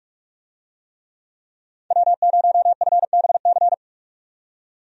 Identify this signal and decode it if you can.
Morse code, heard as a keyed tone, about 44 wpm, 700 Hz. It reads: W0FBC
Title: USA Call Signs - 5 Characters - 45wpm